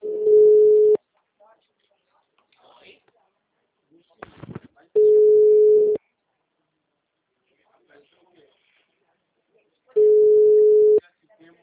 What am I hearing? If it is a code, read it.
Ringback tone (Europe, 425 Hz), with speech.